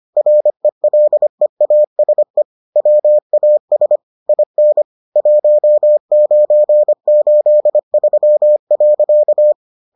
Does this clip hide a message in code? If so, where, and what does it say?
Yes: Morse code from 0.2 to 9.5 s, heard as a keyed tone, about 25 wpm, 600 Hz. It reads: RELEASE WAS IN 1983.